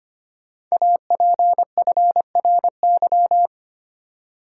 Morse code, heard as a keyed tone, about 25 wpm, 700 Hz. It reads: APFRY
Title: Code Groups - 5 Letters - 25wpm